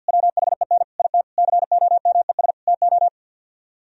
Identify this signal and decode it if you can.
Morse code, heard as a keyed tone, about 60 wpm, 690 Hz. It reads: WHEN IT COMES TO
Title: Speed-Racing - Sets of 5 Words 50wpm